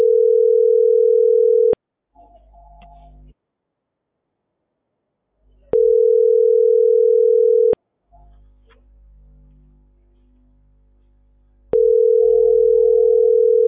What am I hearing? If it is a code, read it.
Ringback tone (North America), 440 + 480 Hz.